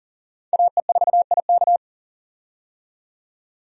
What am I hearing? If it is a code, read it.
Morse code, heard as a keyed tone, about 40 wpm, 700 Hz. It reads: AE4IK
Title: USA Call Signs - 5 Characters - 40wpm